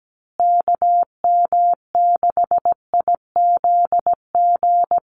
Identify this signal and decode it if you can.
Morse code, heard as a keyed tone, about 17 wpm, 700 Hz. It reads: KM6IZG
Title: USA Call Signs Repeated 3x 17wpm